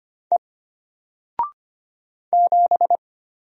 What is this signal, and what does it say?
Morse code, heard as a keyed tone, about 25 wpm, 700 Hz. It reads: E 7
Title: Single Letter-Number Alt 25wpm